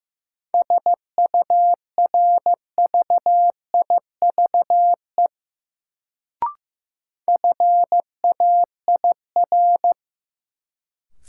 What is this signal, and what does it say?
Morse code, heard as a keyed tone, about 15 wpm, 700 Hz. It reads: SURVIVE FAIR
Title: Sentences from Top 1300 Words - New Word Drill Alt - 15wpm